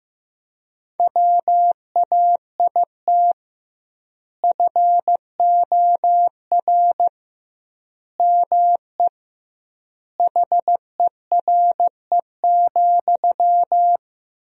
Morse code, heard as a keyed tone, about 15 wpm, 700 Hz. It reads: WAIT FOR ME HERE,
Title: 2x Word-Spacing - Sentences from Top 500 Words - 15wpm